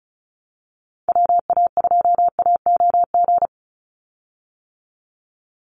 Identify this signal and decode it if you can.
Morse code, heard as a keyed tone, about 34 wpm, 700 Hz. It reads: WA2AOG
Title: Call Signs - 35wpm